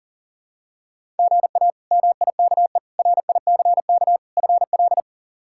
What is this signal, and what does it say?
Morse code, heard as a keyed tone, about 40 wpm, 700 Hz. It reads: GA MIKE RICK FL
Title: K1USN SST QSO Practice - 40wpm